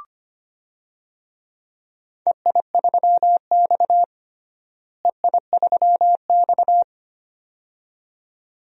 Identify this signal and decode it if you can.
Morse code, heard as a keyed tone, about 25 wpm, 700 Hz. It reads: EI3X EI3X
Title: DXCC-100 Call Signs - 4 Characters - Repeated 2x - 25wpm